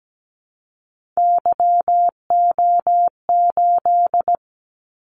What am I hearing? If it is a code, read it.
Morse code, heard as a keyed tone, about 17 wpm, 700 Hz. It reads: YO8
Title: Callsign Prefixes for Top 25 DXCC - 17wpm